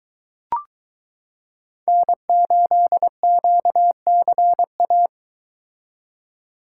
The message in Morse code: N8QCA

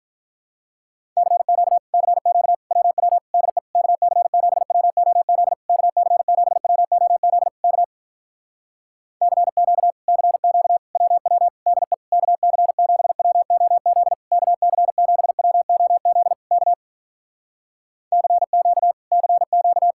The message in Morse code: CQ CQ WW DE KC7WOZ KC7WOZ K CQ CQ WW DE KC7WOZ KC7WOZ K CQ CQ

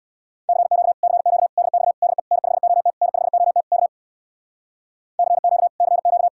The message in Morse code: CQ CQ DX DE N5OT N5OT K CQ CQ

